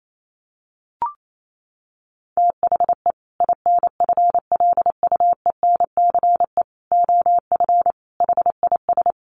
Morse code, heard as a keyed tone, about 28 wpm, 700 Hz. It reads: THE INFLUENCE OF HIS
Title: Sentences from Top 1100 Words 28wpm